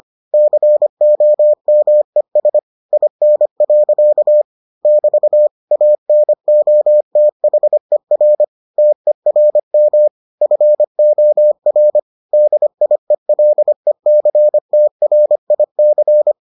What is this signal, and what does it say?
Morse code, heard as a keyed tone, about 25 wpm, 600 Hz. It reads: COMES IN. <BT> ANOTHER TERM FOR DIELECTRIC